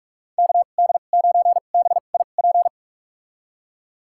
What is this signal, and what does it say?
Morse code, heard as a keyed tone, about 46 wpm, spaced as for 33 wpm, 700 Hz. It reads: KD9BIP
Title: USA Call Signs - 6 Characters - 2x Character Spacing - 45wpm